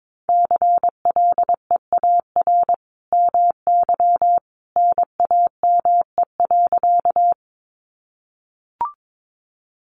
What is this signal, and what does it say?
Morse code, heard as a keyed tone, about 22 wpm, 700 Hz. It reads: CLEAR MY NAME.
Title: Sentences from Top 500 Words 22wpm